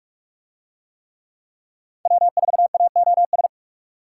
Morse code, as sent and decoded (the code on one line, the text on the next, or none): .-- ....- .- --- ...
W4AOS